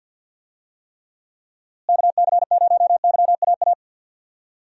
Morse code, heard as a keyed tone, about 50 wpm, 700 Hz. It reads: KC0YAA
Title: USA Call Signs - 6 Characters - Repeated 3x - 50wpm